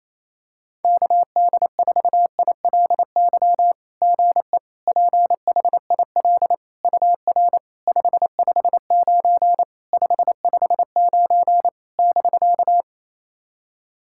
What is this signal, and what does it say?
Morse code, heard as a keyed tone, about 29 wpm, 690 Hz. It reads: KD4ILY GE PHIL UR 559 559 <BK>
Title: SOTA QSO Practice 28wpm